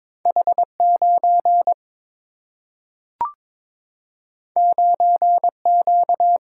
Morse code, heard as a keyed tone, about 22 wpm, 700 Hz. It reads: H 9 9 Q
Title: Single-Letter Single-Number 22wpm